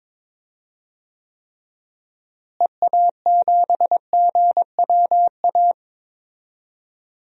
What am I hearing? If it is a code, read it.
Morse code, heard as a keyed tone, about 21 wpm, 700 Hz. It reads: EA7GWA
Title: DXCC-100 Call Signs - 6 Characters - Repeated 3x - 22wpm